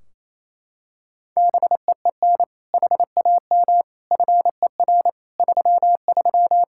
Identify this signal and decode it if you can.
Morse code, heard as a keyed tone, about 28 wpm, 700 Hz. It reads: BEEN HAM FER 33
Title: Practice QSO - Exchange 3 - More Info - 28wpm